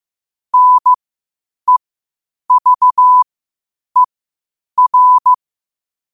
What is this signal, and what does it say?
Morse code, heard as a keyed tone, about 15 wpm, spaced as for 8 wpm, 1000 Hz. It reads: NEVER
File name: qotdfiles / 8wpm